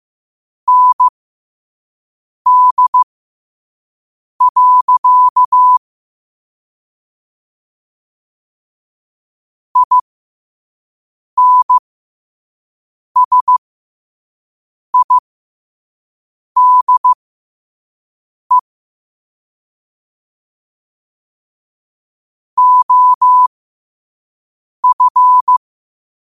Morse code, heard as a keyed tone, about 15 wpm, spaced as for 5 wpm, 1000 Hz. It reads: ND. INSIDE OF
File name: qotdfiles / 5wpm